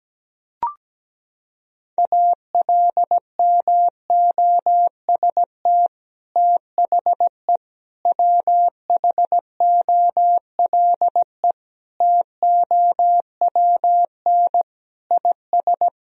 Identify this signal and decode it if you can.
Morse code, heard as a keyed tone, about 17 wpm, 700 Hz. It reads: ALMOST THE WHOLE TOWN IS
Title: Sentences from Top 1000 Words - Review Part 4 - 17wpm